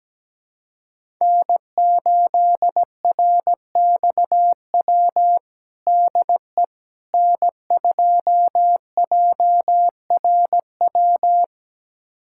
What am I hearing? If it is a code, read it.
Morse code, heard as a keyed tone, about 17 wpm, 700 Hz. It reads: N8RXW DE N2JRW
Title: USA Callsign DE Callsign - 5 Characters - 17wpm